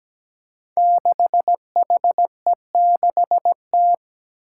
Morse code, heard as a keyed tone, about 17 wpm, 700 Hz. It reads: 6HE6T
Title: Five Challenging Letter-Number Combinations 17wpm